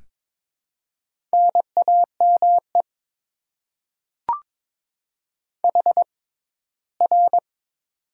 Morse code, heard as a keyed tone, about 22 wpm, spaced as for 15 wpm, 700 Hz. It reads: NAME HR
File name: Top 90 QSO Elements - 6x Character Spacing - 22wpm